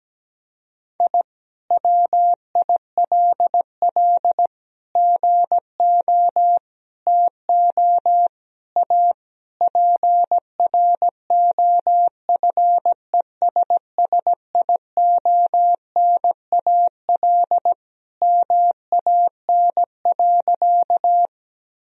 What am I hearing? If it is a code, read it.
Morse code, heard as a keyed tone, about 17 wpm, 700 Hz. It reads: I WILL GO TO A PROFESSIONAL MAN.
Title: Sentences from Top 1000 Words - Review Part 4 - 17wpm